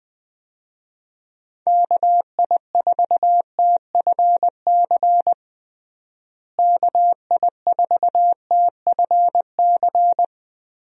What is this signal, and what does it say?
Morse code, heard as a keyed tone, about 21 wpm, 700 Hz. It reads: KI4TFC KI4TFC
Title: USA Call Signs - 6 Characters - Repeated 3x - 20wpm